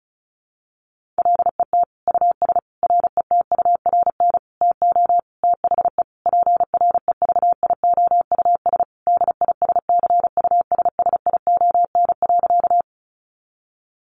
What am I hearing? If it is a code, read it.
Morse code, heard as a keyed tone, about 34 wpm, 700 Hz. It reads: LET US RETURN TO THE PREVIOUS DISCUSSION.
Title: Sentences from Top 1500 Words 35wpm